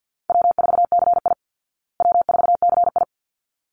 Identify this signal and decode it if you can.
Morse code, heard as a keyed tone, about 50 wpm, 700 Hz. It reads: W4CI W4CI
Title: Call Signs Repeated 3x - 50wpm